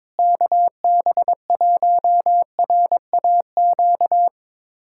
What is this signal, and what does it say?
Morse code, heard as a keyed tone, about 22 wpm, 700 Hz. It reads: KB1RAQ
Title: USA Callsign DE Callsign 22wpm